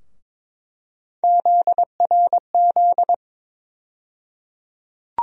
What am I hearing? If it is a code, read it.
Morse code, heard as a keyed tone, about 22 wpm, 700 Hz. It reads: ZRZ
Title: USA Call Sign Postfixes - 3 Letters 22wpm